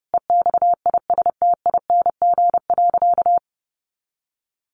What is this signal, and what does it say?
Morse code, heard as a keyed tone, about 30 wpm, 700 Hz. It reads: EXISTING.
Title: Sentences from Top 1500 Words 30wpm